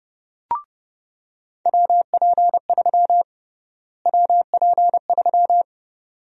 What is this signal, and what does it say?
Morse code, heard as a keyed tone, about 30 wpm, 700 Hz. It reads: WP3 WP3
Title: USA Call Sign Prefixes - Repeated 2x - 30wpm